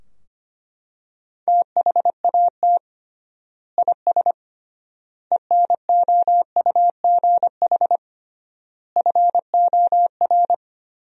Morse code, heard as a keyed tone, about 25 wpm, 700 Hz. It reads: THAT IS ENOUGH FOR
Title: 3x Word-Spacing - Sentences from Top 400 Words - 25wpm